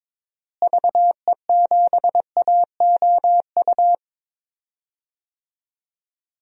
Morse code, heard as a keyed tone, about 22 wpm, 700 Hz. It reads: VE7AOU
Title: DXCC-100 Call Signs - 6 Characters - Repeated 2x - 22wpm